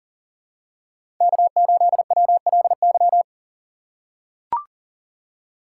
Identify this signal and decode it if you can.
Morse code, heard as a keyed tone, about 40 wpm, 700 Hz. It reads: K8WLY
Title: USA Call Signs - 5 Characters - 3x Character Spacing - 40wpm